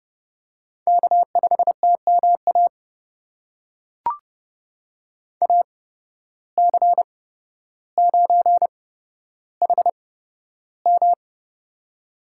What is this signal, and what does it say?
Morse code, heard as a keyed tone, about 30 wpm, spaced as for 13 wpm, 700 Hz. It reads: K5TMA AC9HM
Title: USA Call Signs - 5 Characters - 8x Character Spacing - 30wpm